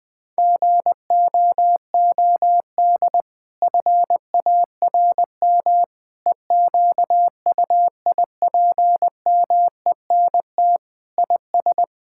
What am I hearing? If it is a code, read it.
Morse code, heard as a keyed tone, about 20 wpm, 700 Hz. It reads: GOOD FARM EQUIPMENT IS
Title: Sentences from Top 1000 Words - Review Part 5 - 20wpm